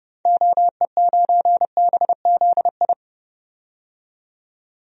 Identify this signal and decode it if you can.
Morse code, heard as a keyed tone, about 30 wpm, 700 Hz. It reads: OE9BZI
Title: DXCC-100 Call Signs - 6 Characters - 30wpm